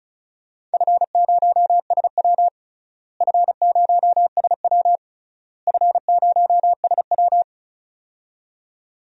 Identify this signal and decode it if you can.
Morse code, heard as a keyed tone, about 36 wpm, 700 Hz. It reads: F0SW F0SW F0SW
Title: DXCC-100 Call Signs - 4 Characters - Repeated 3x - 35wpm